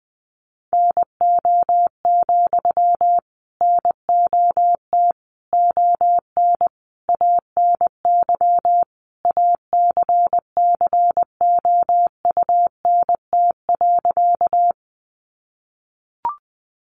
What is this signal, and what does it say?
Morse code, heard as a keyed tone, about 20 wpm, 700 Hz. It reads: NO, NOT ON ANY ACCOUNT.
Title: Sentences from Top 900 Words 20wpm